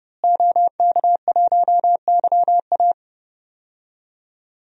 Morse code, heard as a keyed tone, about 31 wpm, 700 Hz. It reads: OK1YA
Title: DXCC-100 Call Signs - 5 Characters - Repeated 2x - 30wpm